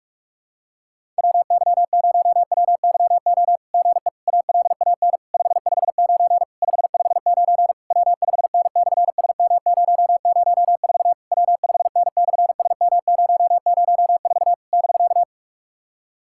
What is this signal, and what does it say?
Morse code, heard as a keyed tone, about 46 wpm, 690 Hz. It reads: WY0WYO GE ALAN 559 559 W5N/SM004 W5N/SM004 <BK>
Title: SOTA QSO Practice 45wpm